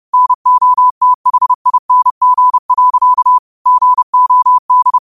Morse code, heard as a keyed tone, about 30 wpm, 1000 Hz. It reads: NOTHING. GOD